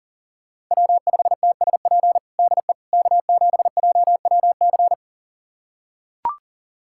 Morse code, heard as a keyed tone, about 40 wpm, 700 Hz. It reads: W5TSP DE K7JWC
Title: USA Callsign DE Callsign - 5 Characters - 40wpm